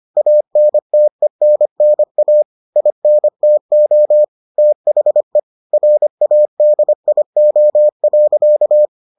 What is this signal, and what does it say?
Morse code, heard as a keyed tone, about 25 wpm, 600 Hz. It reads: ANTENNA INTO THE RADIO.